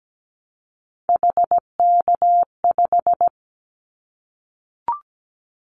Morse code, heard as a keyed tone, about 17 wpm, 700 Hz. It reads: HK5
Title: Callsign Prefixes for Top 100 DXCC - 17wpm